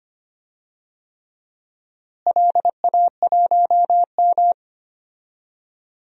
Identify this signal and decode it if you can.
Morse code, heard as a keyed tone, about 25 wpm, 700 Hz. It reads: LA1M
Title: DXCC-100 Call Signs - 4 Characters - Repeated 3x - 25wpm